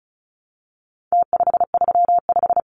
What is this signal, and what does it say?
Morse code, heard as a keyed tone, about 34 wpm, 700 Hz. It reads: T535
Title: Numbers - Contest Style - 4-Digit 35wpm